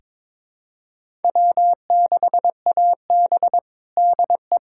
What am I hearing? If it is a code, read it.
Morse code, heard as a keyed tone, about 22 wpm, 700 Hz. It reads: W6AB DE
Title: USA Callsign DE Callsign - 4 Characters - 22wpm